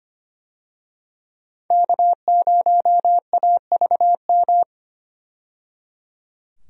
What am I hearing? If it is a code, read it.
Morse code, heard as a keyed tone, about 25 wpm, 700 Hz. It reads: K0AVM
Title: USA Call Signs - 5 Characters - 25wpm